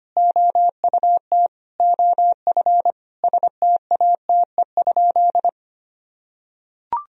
Morse code, heard as a keyed tone, about 25 wpm, 700 Hz. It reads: OUT OF STATE?
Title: Sentences from Top 1000 Words - Review Part 2 - 25wpm